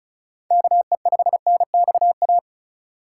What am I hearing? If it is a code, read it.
Morse code, heard as a keyed tone, about 36 wpm, 700 Hz. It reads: KE5NXA
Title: USA Call Signs - 6 Characters - 3x Character Spacing - 35wpm